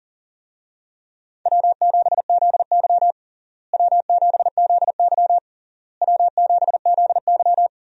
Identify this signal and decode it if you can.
Morse code, heard as a keyed tone, about 37 wpm, 700 Hz. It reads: W7ZY W7ZY W7ZY
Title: USA Call Signs - 4 Characters - Repeated 3x - 40wpm